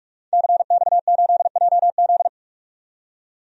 Morse code, heard as a keyed tone, about 47 wpm, 700 Hz. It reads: CX8JZ
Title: DXCC-100 Call Signs - 5 Characters - 45wpm